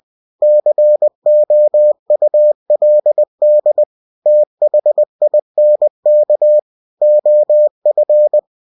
Morse code, heard as a keyed tone, about 20 wpm, 600 Hz. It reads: COULD THINK OF
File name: cw-books / winnie-the-pooh / winnie-20wpm-page-0036